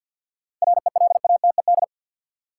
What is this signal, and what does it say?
Morse code, heard as a keyed tone, about 53 wpm, 700 Hz. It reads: RELATED
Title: Sentences from Top 1400 Words - New Word Drill Alt - 50wpm